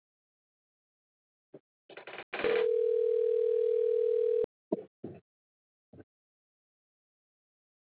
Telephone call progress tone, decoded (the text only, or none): ringback tone